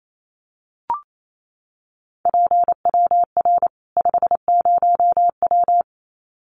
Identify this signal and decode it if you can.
Morse code, heard as a keyed tone, about 28 wpm, 700 Hz. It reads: PWR 50W
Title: Common QSO Phrases 28wpm